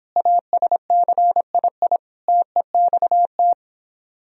Morse code, heard as a keyed tone, about 26 wpm, 700 Hz. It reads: ASCII TEXT